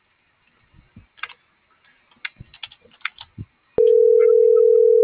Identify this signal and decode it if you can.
Ringback tone (North America), 440 + 480 Hz.